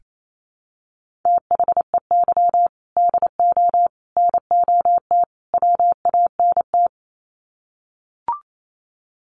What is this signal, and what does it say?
Morse code, heard as a keyed tone, about 28 wpm, 700 Hz. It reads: THEY DO NOT WANT
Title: Sets of 4 Words 28wpm